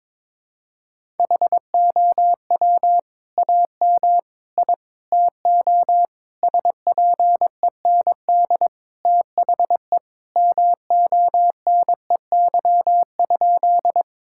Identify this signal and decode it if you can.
Morse code, heard as a keyed tone, about 22 wpm, 700 Hz. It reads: HOW AM I TO SPEND THE MONEY?
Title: Sentences from Top 1000 Words - Review Part 5 - 22wpm